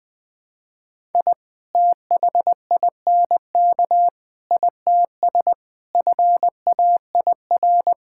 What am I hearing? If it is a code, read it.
Morse code, heard as a keyed tone, about 20 wpm, 700 Hz. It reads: I THINK ITS FAIR
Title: Sentences with 7-Word Phrases - Spoken Beginning - 20wpm